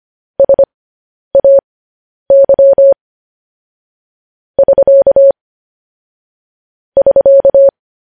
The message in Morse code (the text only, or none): SAY <SK> <SK>